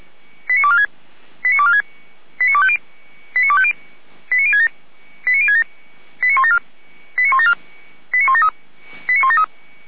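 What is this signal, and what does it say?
Five-tone selective call (ZVEI), heard as quick runs of five tones, tones of about 70 ms. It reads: 89246 89246 89250 89250 89067 89067 89174 89174 89173 89173
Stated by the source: ZVEI five-tone call 89108 received on SDR, feuerwehr-orlamuende.org fire service.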